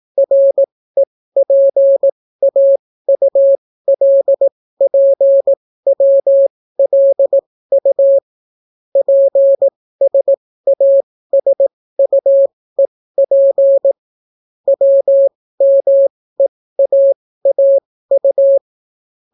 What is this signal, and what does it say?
Morse code, heard as a keyed tone, about 18 wpm, spaced as for 14 wpm, 550 Hz. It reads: REPAULPWLU PSASUEP WMEAAU